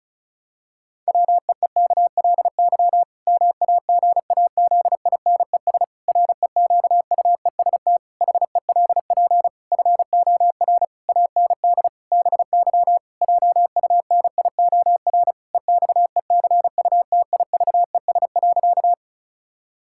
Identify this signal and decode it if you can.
Morse code, heard as a keyed tone, about 34 wpm, 700 Hz. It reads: WEEKLY MAGAZINES REQUEST HELP FOR AND BY JUNIOR EXECUTIVES.
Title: Pangram Challenge v2 - 35wpm